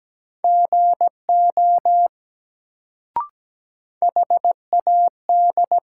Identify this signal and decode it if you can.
Morse code, heard as a keyed tone, about 17 wpm, 700 Hz. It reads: GO HAD
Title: Sentences from Top 100 Words - New Word Drill - 17wpm